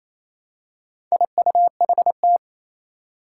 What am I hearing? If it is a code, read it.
Morse code, heard as a keyed tone, about 29 wpm, 700 Hz. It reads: IUHT
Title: Four Challenging Letter-Number Combinations 28wpm